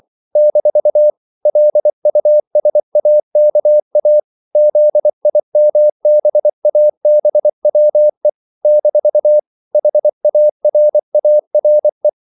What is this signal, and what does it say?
Morse code, heard as a keyed tone, about 24 wpm, 600 Hz. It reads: - LUSAKA ZIMBABWE - HARARE
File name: CW / Practice / NationalCapitals24